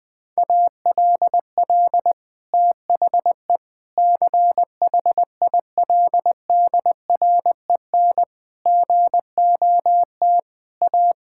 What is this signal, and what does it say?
Morse code, heard as a keyed tone, about 20 wpm, 700 Hz. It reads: ALL THE CHILDREN GOT A
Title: Sentences from Top 1000 Words - Review Part 6 - 20wpm